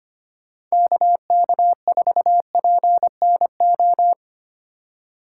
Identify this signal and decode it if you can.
Morse code, heard as a keyed tone, about 25 wpm, 700 Hz. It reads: KK4PNO
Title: USA Call Signs - 6 Characters - 4x Character Spacing - 25wpm